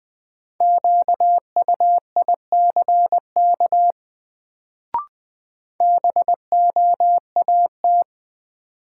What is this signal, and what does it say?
Morse code, heard as a keyed tone, about 20 wpm, 700 Hz. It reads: QUICK BOAT
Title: Sentences from Top 500 Words - New Word Drill - 20wpm